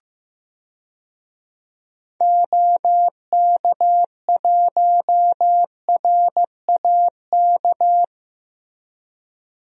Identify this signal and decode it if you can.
Morse code, heard as a keyed tone, about 15 wpm, 700 Hz. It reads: OK1RAK
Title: DXCC-100 Call Signs - 6 Characters - Repeated 3x - 15wpm